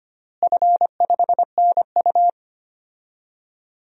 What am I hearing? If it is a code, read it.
Morse code, heard as a keyed tone, about 25 wpm, 700 Hz. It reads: F5NU
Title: DXCC-100 Call Signs - 4 Characters - Repeated 3x - 25wpm